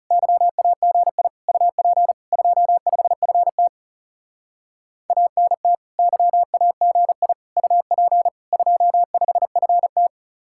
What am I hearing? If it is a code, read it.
Morse code, heard as a keyed tone, about 37 wpm, 700 Hz. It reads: YAGI UP 25FT ANT YAGI UP 25FT
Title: Speed-Racing Common QSO Phrases - 30wpm